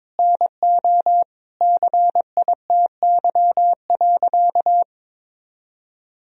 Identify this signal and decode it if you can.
Morse code, heard as a keyed tone, about 21 wpm, 700 Hz. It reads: NO CITY.
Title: Sentences from Top 1000 Words - Review Part 1 - 22wpm